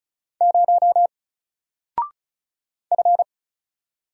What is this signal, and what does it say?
Morse code, heard as a keyed tone, about 36 wpm, 700 Hz. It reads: 0 F
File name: Single-Letter-Number-35wpm